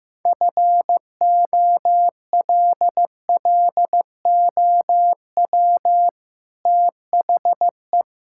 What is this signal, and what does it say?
Morse code, heard as a keyed tone, about 15 wpm, 700 Hz. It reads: FOLLOW THE
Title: Sentences from Top 1000 Words - Review Part 5 - 15wpm